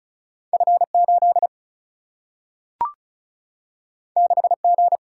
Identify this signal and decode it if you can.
Morse code, heard as a keyed tone, about 36 wpm, 700 Hz. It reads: F8 6G
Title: Single-Letter Single-Number 35wpm